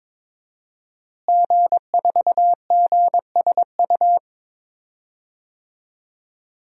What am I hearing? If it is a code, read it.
Morse code, heard as a keyed tone, about 22 wpm, 700 Hz. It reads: G4GSU
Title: DXCC-100 Call Signs - 5 Characters - Repeated 3x - 22wpm